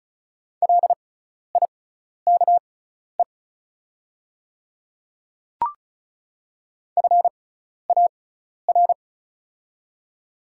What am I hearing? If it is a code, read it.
Morse code, heard as a keyed tone, about 36 wpm, spaced as for 12 wpm, 700 Hz. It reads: LIKE FAR
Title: 6x Character Spacing - Top 300 Words - 35wpm